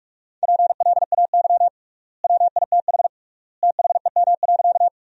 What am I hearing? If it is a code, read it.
Morse code, heard as a keyed tone, about 46 wpm, 700 Hz. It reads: PLAY WITH THEM.
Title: 3x Word-Spacing - Sentences from Top 200 Words - 45wpm